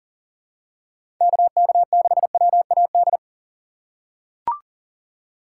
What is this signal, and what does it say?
Morse code, heard as a keyed tone, about 40 wpm, 700 Hz. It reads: KK6WAD
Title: USA Call Signs - 6 Characters - 3x Character Spacing - 40wpm